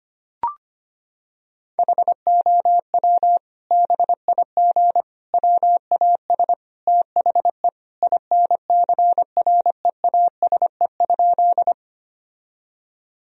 Morse code, heard as a keyed tone, about 25 wpm, 700 Hz. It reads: HOW BIG WAS THE INCREASE?
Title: Sentences from Top 1000 Words - Review Part 4 - 25wpm